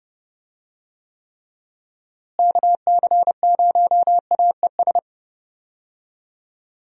Morse code, heard as a keyed tone, about 32 wpm, 700 Hz. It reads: KC0AES
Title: USA Call Signs - 6 Characters - Repeated 3x - 30wpm